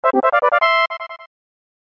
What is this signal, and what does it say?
There is no code.